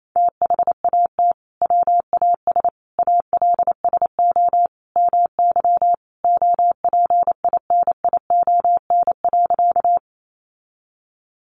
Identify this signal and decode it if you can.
Morse code, heard as a keyed tone, about 28 wpm, 700 Hz. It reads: THAT WAS ALSO MY OPINION.
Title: Sentences from Top 1200 Words 28wpm